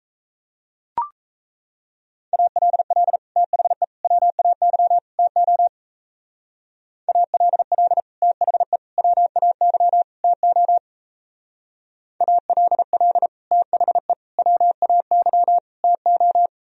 Morse code, heard as a keyed tone, about 34 wpm, 700 Hz. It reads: ALL THE WAY TO ALL THE WAY TO ALL THE WAY TO
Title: Speed-Racing - Sets of 4 Words 28wpm